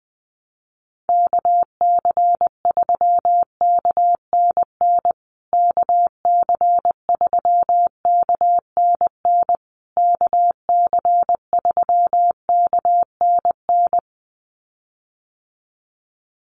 Morse code, heard as a keyed tone, about 20 wpm, 700 Hz. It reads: KC3KNN KC3KNN KC3KNN
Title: USA Call Signs Repeated 3x 20wpm